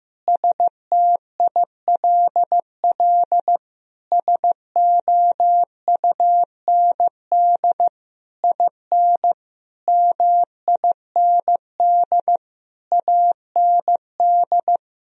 Morse code, heard as a keyed tone, about 15 wpm, 700 Hz. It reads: STILL SOUND IN MIND AND
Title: Sentences from Top 1000 Words - Review Part 2 - 15wpm